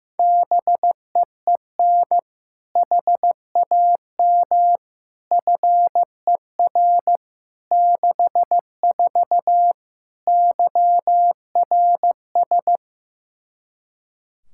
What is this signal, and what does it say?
Morse code, heard as a keyed tone, about 15 wpm, 700 Hz. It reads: BEEN HAM FER 64 YRS
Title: K1USN SST QSO Practice 15wpm